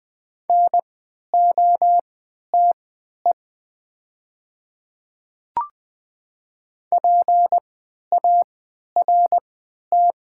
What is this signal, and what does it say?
Morse code, heard as a keyed tone, about 20 wpm, spaced as for 11 wpm, 700 Hz. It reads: NOTE PART
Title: 3x Character Spacing - Top 500 Words - 20wpm